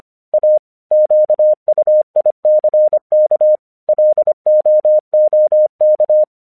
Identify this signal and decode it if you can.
Morse code, heard as a keyed tone, about 25 wpm, 610 Hz. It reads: A QUICK LOOK
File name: English 25wpm